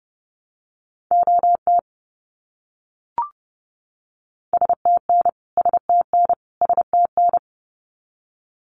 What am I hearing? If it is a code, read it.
Morse code, heard as a keyed tone, about 30 wpm, 700 Hz. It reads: OT STN STN STN
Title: Abbreviations Repeated 3x - 30wpm